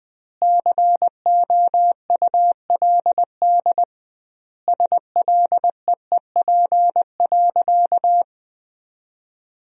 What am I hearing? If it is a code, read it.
Morse code, heard as a keyed tone, about 20 wpm, 700 Hz. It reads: COULD SLEEP.
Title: 2x Word-Spacing - Sentences from Top 500 Words - 20wpm